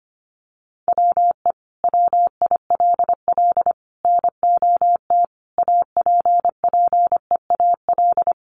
Morse code, heard as a keyed tone, about 25 wpm, 700 Hz. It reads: WE WILL NOT APPEAL
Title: Sentences from Top 1300 Words 25wpm